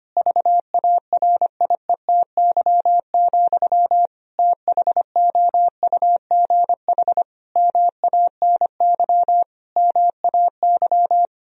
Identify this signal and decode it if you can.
Morse code, heard as a keyed tone, about 25 wpm, 700 Hz. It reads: VARIETY, THOUGH MANY MAY
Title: Sentences from Top 1000 Words - Review Part 4 - 25wpm